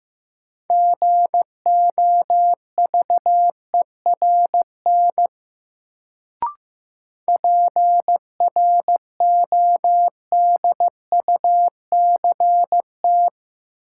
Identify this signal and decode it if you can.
Morse code, heard as a keyed tone, about 15 wpm, 700 Hz. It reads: GOVERN PRODUCT
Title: Sentences from Top 1000 Words - Review Part 2 - New Word Drill - 15wpm